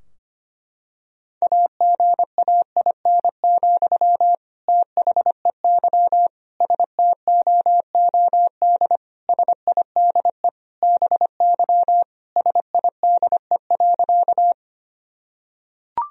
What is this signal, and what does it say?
Morse code, heard as a keyed tone, about 25 wpm, 700 Hz. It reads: AGAIN, THEY STOOD SIDE BY SIDE.
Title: Sentences from Top 1000 Words - Review Part 2 - 25wpm